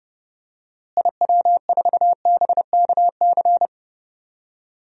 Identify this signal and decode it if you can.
Morse code, heard as a keyed tone, about 30 wpm, 700 Hz. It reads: IW4BKC
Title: DXCC-100 Call Signs - 6 Characters - Repeated 2x - 30wpm